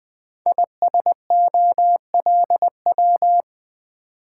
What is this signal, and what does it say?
Morse code, heard as a keyed tone, about 20 wpm, 700 Hz. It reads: ISOLW
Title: Code Groups - 5 Letter:Numbers - 20wpm